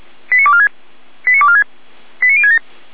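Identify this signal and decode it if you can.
Five-tone selective call (ZVEI), heard as quick runs of five tones, tones of about 80 ms. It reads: 89246 89246 89067
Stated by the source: ZVEI five-tone selective call 89168 received by SDR.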